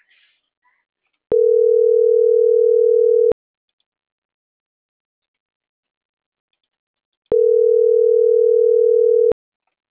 Ringback tone (North America), 440 + 480 Hz.
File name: CENTRAL NAVIERAS / out roberto.rueda